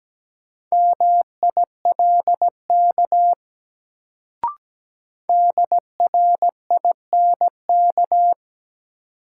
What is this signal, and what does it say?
Morse code, heard as a keyed tone, about 17 wpm, 700 Hz. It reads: MILK DRINK
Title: Sentences from Top 1500 Words - New Word Drill Alt - 17wpm